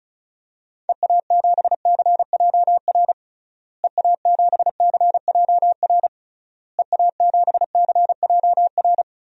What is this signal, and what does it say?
Morse code, heard as a keyed tone, about 36 wpm, 700 Hz. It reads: EA7CJR EA7CJR EA7CJR
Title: DXCC-100 Call Signs - 6 Characters - Repeated 3x - 35wpm